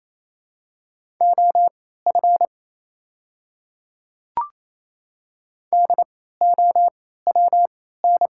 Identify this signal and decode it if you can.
Morse code, heard as a keyed tone, about 27 wpm, spaced as for 16 wpm, 700 Hz. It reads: OF DOWN
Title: 3x Character Spacing - Top 100 Words - 28wpm